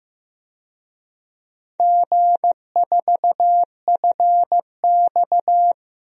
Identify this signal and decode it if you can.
Morse code, heard as a keyed tone, about 15 wpm, 700 Hz. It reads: G4FX
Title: DXCC-100 Call Signs - 4 Characters - Repeated 3x - 15wpm